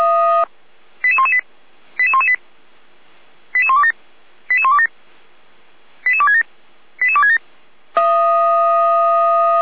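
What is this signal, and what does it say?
Five-tone selective call (ZVEI), heard as quick runs of five tones, tones of about 70 ms. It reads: 88118 88118 88127 88127 88267 88267
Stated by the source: ZVEI five-tone selective call received by SDR, logged on the Feuerwehr Orlamünde server.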